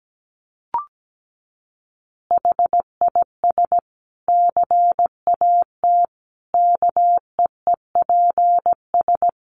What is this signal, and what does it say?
Morse code, heard as a keyed tone, about 17 wpm, 700 Hz. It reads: HIS CAT KEEPS
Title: Additional Easy Phrases 17wpm